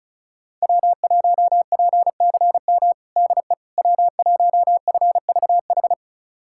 Morse code, heard as a keyed tone, about 36 wpm, 700 Hz. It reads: W1PCM DE W1FVH
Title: USA Callsign DE Callsign - 5 Characters - 35wpm